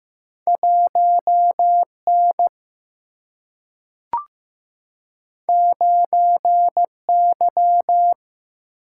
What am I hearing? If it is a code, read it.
Morse code, heard as a keyed tone, about 15 wpm, 700 Hz. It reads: 1 N 9 Y
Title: Single-Letter Single-Number 15wpm